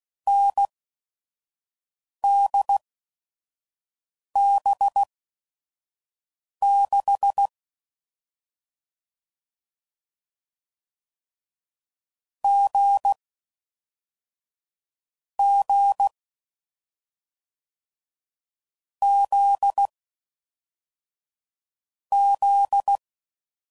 Morse code, heard as a keyed tone, about 16 wpm, spaced as for 5 wpm, 800 Hz. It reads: NDB6 G G Z Z